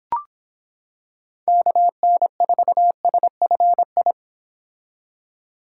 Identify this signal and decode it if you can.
Morse code, heard as a keyed tone, about 26 wpm, 700 Hz. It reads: KN4SFI